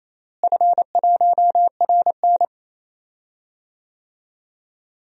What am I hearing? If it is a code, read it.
Morse code, heard as a keyed tone, about 27 wpm, 700 Hz. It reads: F1RN
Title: DXCC-100 Call Signs - 4 Characters - Repeated 2x - 28wpm